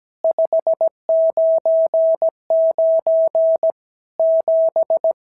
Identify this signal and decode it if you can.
Morse code, heard as a keyed tone, about 17 wpm, 650 Hz. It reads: 599 7
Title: SOTA QSO Practice 17wpm